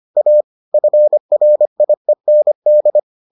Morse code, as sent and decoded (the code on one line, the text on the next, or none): .- / ..-. .-. .. -. -..
A FRIEND